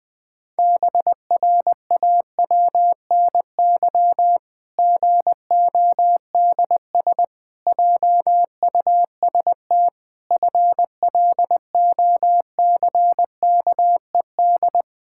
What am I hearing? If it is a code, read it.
Morse code, heard as a keyed tone, about 20 wpm, 700 Hz. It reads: BRAWNY GODS JUST FLOCKED
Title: Pangram Challenge v2 - 20wpm